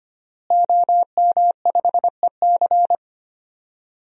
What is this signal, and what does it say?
Morse code, heard as a keyed tone, about 25 wpm, 700 Hz. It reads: OM5EC
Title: DXCC-100 Call Signs - 5 Characters - Repeated 3x - 25wpm